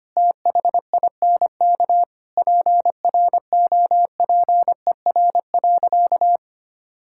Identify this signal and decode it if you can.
Morse code, heard as a keyed tone, about 25 wpm, 700 Hz. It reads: THINK PROPER.
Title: Sentences from Top 1000 Words - Review Part 5 - 25wpm